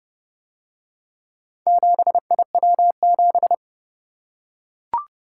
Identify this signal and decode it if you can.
Morse code, heard as a keyed tone, about 28 wpm, 700 Hz. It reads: 7IW7